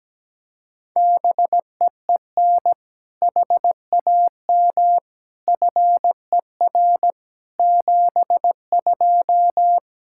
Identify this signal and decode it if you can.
Morse code, heard as a keyed tone, about 17 wpm, 700 Hz. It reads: BEEN HAM FER 72
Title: K1USN SST QSO Practice 17wpm